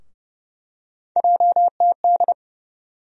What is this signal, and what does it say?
Morse code, heard as a keyed tone, about 29 wpm, 700 Hz. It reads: JTD
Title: USA Call Sign Postfixes - 3 Letters 30wpm